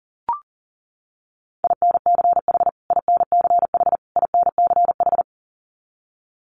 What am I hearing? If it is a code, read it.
Morse code, heard as a keyed tone, about 40 wpm, 700 Hz. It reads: INCH INCH INCH
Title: Top 500 Words Repeated 40wpm